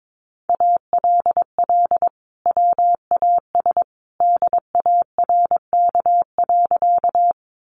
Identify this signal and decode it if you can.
Morse code, heard as a keyed tone, about 22 wpm, 700 Hz. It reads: ALL WAS DARK.
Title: Sentences from Top 500 Words 22wpm